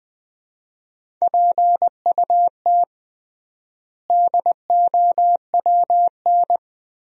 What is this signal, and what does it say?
Morse code, heard as a keyed tone, about 20 wpm, 700 Hz. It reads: PUT DOWN
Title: 3x Word-Spacing - Sentences from Top 400 Words - 20wpm